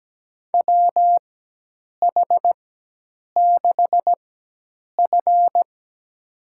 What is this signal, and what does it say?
Morse code, heard as a keyed tone, about 17 wpm, spaced as for 8 wpm, 700 Hz. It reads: WH6F
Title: USA Call Signs - 4x Character Spacing - 17wpm